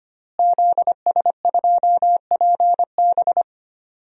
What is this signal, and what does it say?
Morse code, heard as a keyed tone, about 25 wpm, 700 Hz. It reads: ZS2PB
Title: DXCC-100 Call Signs - 5 Characters - Repeated 2x - 25wpm